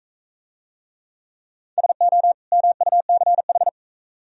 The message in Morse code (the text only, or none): SO MUCH